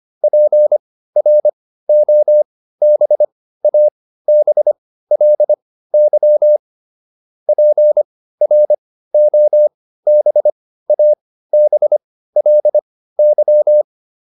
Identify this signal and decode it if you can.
Morse code, heard as a keyed tone, about 25 wpm, spaced as for 15 wpm, 600 Hz. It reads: PROBABLY PROBABLY